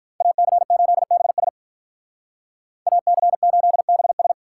Morse code, heard as a keyed tone, about 45 wpm, 700 Hz. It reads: AC8BS AC8BS